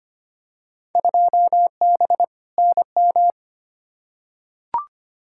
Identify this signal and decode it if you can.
Morse code, heard as a keyed tone, about 25 wpm, 700 Hz. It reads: 2B NM
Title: Field Day Exchanges Encore - 25wpm